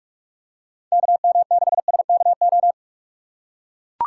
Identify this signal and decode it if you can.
Morse code, heard as a keyed tone, about 43 wpm, 700 Hz. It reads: KM6SKO
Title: USA Call Signs - 6 Characters - 6x Character Spacing - 45wpm